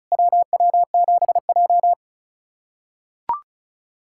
Morse code, heard as a keyed tone, about 36 wpm, 690 Hz. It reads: WW7J
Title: SOTA QSO Practice 35wpm